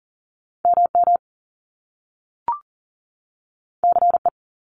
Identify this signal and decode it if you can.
Morse code, heard as a keyed tone, about 38 wpm, 700 Hz. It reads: MM CE
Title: Top Two Letter Combinations in Words 40wpm